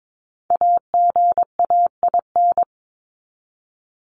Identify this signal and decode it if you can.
Morse code, heard as a keyed tone, about 22 wpm, 700 Hz. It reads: AGAIN
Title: Top 200 Words 22wpm